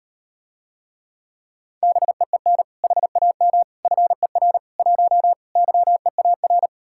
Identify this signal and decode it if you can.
Morse code, heard as a keyed tone, about 39 wpm, 700 Hz. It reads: BEEN HAM FER 1 YEAR